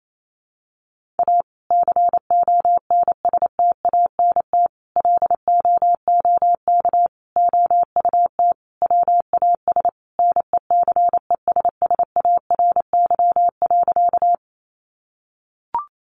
Morse code, heard as a keyed tone, about 28 wpm, 700 Hz. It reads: A CONSTANT LOOK OUT WAS NECESSARY.
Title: Sentences from Top 1500 Words 28wpm